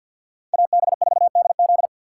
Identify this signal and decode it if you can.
Morse code, heard as a keyed tone, about 53 wpm, 700 Hz. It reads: AB4DZ
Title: USA Call Signs - 6x Character Spacing - 50wpm